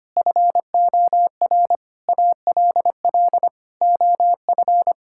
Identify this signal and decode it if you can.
Morse code, heard as a keyed tone, about 25 wpm, 700 Hz. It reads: FOR ALL OF
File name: practice / 25wpm / Sentences with 4-Word Phrases - 25wpm